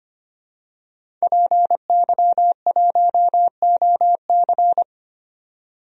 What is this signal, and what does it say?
Morse code, heard as a keyed tone, about 25 wpm, 700 Hz. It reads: PY1OC
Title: DXCC-100 Call Signs - 5 Characters - 25wpm